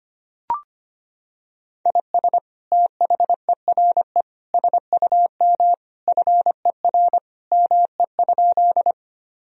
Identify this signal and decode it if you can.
Morse code, heard as a keyed tone, about 25 wpm, 700 Hz. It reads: IS THERE SUM FER ME?
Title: Sentences from Top 200 Words and Abbreviations 25wpm